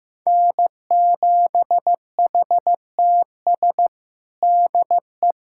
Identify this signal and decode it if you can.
Morse code, heard as a keyed tone, about 15 wpm, 700 Hz. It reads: N7HTS DE
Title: USA Callsign DE Callsign 15wpm